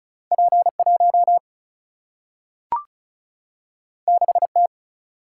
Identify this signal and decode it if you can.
Morse code, heard as a keyed tone, about 36 wpm, 700 Hz. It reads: P 1 6 T
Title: Single-Letter Single-Number 35wpm